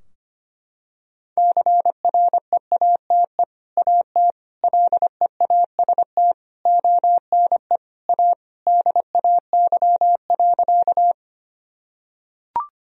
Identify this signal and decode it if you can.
Morse code, heard as a keyed tone, about 25 wpm, 700 Hz. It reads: CREATE AT LEAST ONE A DAY.
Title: Sentences from Top 1000 Words - Review Part 3 - 25wpm